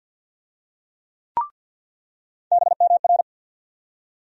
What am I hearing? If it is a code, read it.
Morse code, heard as a keyed tone, about 49 wpm, 700 Hz. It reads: BMR
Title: USA Call Sign Postfixes - 3 Letters 50wpm